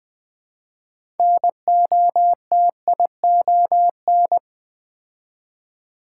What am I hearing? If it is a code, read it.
Morse code, heard as a keyed tone, about 20 wpm, 700 Hz. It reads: NOTION
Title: Build A Word - 3 Letters At A Time - 6-Letter Words - 20wpm